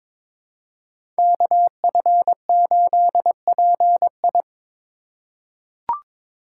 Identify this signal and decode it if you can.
Morse code, heard as a keyed tone, about 22 wpm, 700 Hz. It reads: KF8PI
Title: USA Call Signs - 5 Characters - 2x Character Spacing - 22wpm